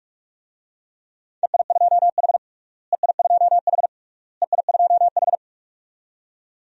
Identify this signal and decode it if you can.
Morse code, heard as a keyed tone, about 46 wpm, 700 Hz. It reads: EI2H EI2H EI2H
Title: DXCC-100 Call Signs - 4 Characters - Repeated 3x - 45wpm